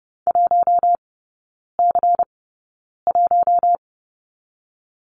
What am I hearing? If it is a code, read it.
Morse code, heard as a keyed tone, about 30 wpm, spaced as for 9 wpm, 700 Hz. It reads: 1C1
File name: Lesson 16e - Letter c - Three Characters 30wpm